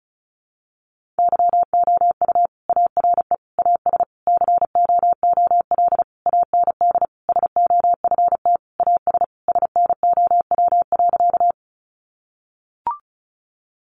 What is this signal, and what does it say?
Morse code, heard as a keyed tone, about 34 wpm, 700 Hz. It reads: YOU ARE AS COOL AND SOFT AS SNOW.
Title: Sentences from Top 1500 Words 35wpm